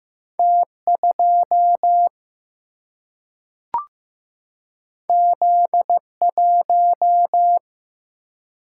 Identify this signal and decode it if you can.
Morse code, heard as a keyed tone, about 15 wpm, 700 Hz. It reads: T2 Z1
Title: Single-Letter Single-Number 15wpm